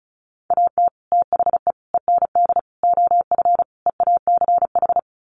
Morse code, heard as a keyed tone, about 34 wpm, 700 Hz. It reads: AT THE END OF EACH
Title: Sets of 5 Words Encore 35wpm